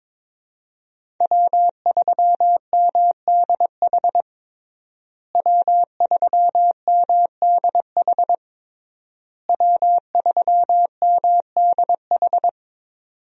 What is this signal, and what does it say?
Morse code, heard as a keyed tone, about 22 wpm, 700 Hz. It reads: W3MDH W3MDH W3MDH
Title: USA Call Signs - 5 Characters - Repeated 3x - 22wpm